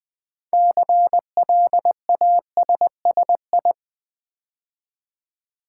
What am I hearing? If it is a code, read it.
Morse code, heard as a keyed tone, about 20 wpm, 700 Hz. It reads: CLASSI
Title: Build A Word - 3 Letters At A Time - 9-Letter Words - 20wpm